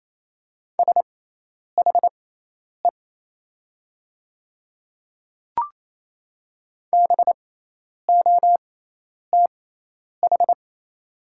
Morse code, heard as a keyed tone, about 27 wpm, spaced as for 10 wpm, 700 Hz. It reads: SHE BOTH
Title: 6x Character Spacing - Top 300 Words - 28wpm